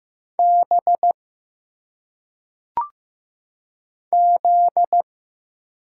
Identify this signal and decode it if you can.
Morse code, heard as a keyed tone, about 15 wpm, 700 Hz. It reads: B Z
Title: Single Letters - Phonetic Pronunciation 15wpm